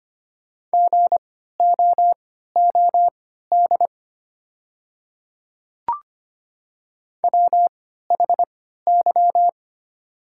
Morse code, heard as a keyed tone, about 25 wpm, spaced as for 14 wpm, 700 Hz. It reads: GOOD WHY
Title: 3x Character Spacing - Top 200 Words - 25wpm